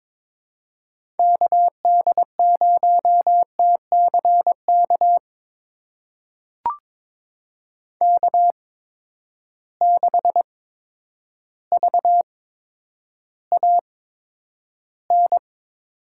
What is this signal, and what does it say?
Morse code, heard as a keyed tone, about 22 wpm, spaced as for 10 wpm, 700 Hz. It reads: KD0TCK K6VAN
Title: USA Call Signs - 8x Character Spacing - 22wpm